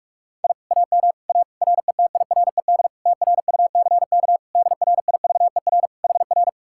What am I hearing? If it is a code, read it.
Morse code, heard as a keyed tone, about 58 wpm, 700 Hz. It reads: I AM A RETIRED TRUCK DRIVER HR